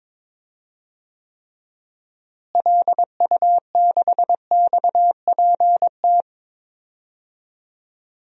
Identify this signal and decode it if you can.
Morse code, heard as a keyed tone, about 22 wpm, 700 Hz. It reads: LU6XPT
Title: DXCC-100 Call Signs - 6 Characters - Repeated 2x - 22wpm